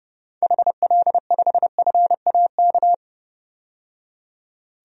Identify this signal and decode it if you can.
Morse code, heard as a keyed tone, about 30 wpm, 700 Hz. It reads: HL5FAK
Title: DXCC-100 Call Signs - 6 Characters - 30wpm